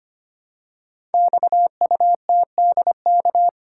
Morse code, heard as a keyed tone, about 25 wpm, 700 Hz. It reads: XUTDK
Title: Code Groups - 5 Letters - 25wpm